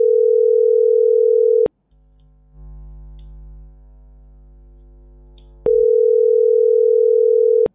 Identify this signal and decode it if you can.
Ringback tone (North America), 440 + 480 Hz.